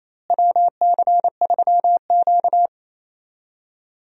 Morse code, heard as a keyed tone, about 28 wpm, 700 Hz. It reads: WC3Q
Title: USA Call Signs - 4 Characters - Repeated 2x - 28wpm